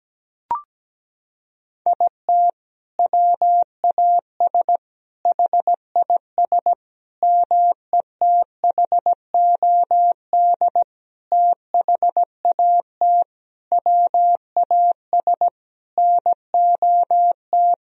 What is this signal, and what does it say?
Morse code, heard as a keyed tone, about 17 wpm, 700 Hz. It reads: IT WAS HIS METHOD THAT WAS NOT
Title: Sentences from Top 1000 Words - Review Part 3 - 17wpm